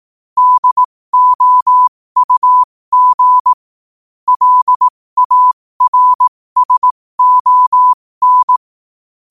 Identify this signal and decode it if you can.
Morse code, heard as a keyed tone, about 18 wpm, spaced as for 15 wpm, 1000 Hz. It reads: DOUG LARSON
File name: qotdfiles / 15wpm